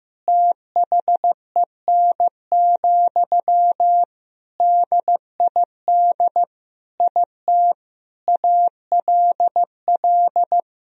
Morse code, heard as a keyed tone, about 16 wpm, 700 Hz. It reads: THEN, DID IT ALL
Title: Sentences from Top 1000 Words - Review Part 4 - 15wpm